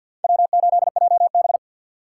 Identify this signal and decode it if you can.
Morse code, heard as a keyed tone, about 52 wpm, 700 Hz. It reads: W8JB
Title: USA Call Signs - 4 Characters - 3x Character Spacing - 50wpm